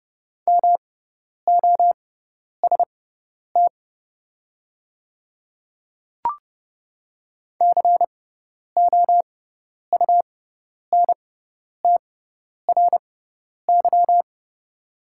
Morse code, heard as a keyed tone, about 31 wpm, spaced as for 10 wpm, 700 Hz. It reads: MOST COUNTRY
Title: 6x Character Spacing - Top 300 Words - 30wpm